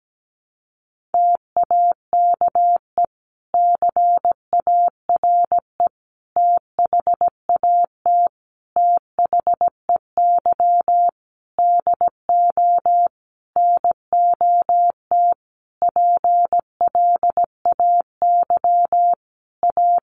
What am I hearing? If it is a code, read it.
Morse code, heard as a keyed tone, about 17 wpm, 700 Hz. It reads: TAKE CARE THAT THEY DO NOT PLAY A
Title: Sentences from Top 1500 Words 17wpm